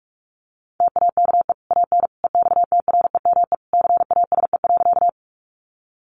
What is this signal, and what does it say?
Morse code, heard as a keyed tone, about 44 wpm, 700 Hz. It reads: TAKE AN EXTREME CASE.
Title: Sentences from Top 1300 Words 45wpm